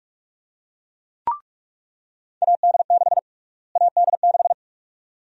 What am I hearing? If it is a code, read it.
Morse code, heard as a keyed tone, about 42 wpm, 700 Hz. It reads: AD6 AD6
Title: USA Call Sign Prefixes - Repeated 2x - 45wpm